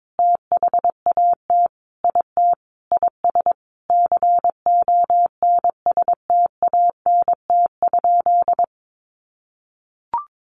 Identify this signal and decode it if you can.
Morse code, heard as a keyed tone, about 22 wpm, 700 Hz. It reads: THAT IT IS CONSTANT?
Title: Sentences from Top 1500 Words 22wpm